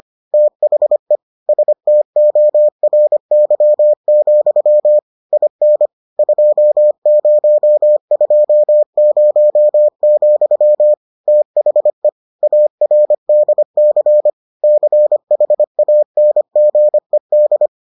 Morse code, heard as a keyed tone, about 25 wpm, 600 Hz. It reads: THE STORY, IN 2020, THE ARDC CHANGED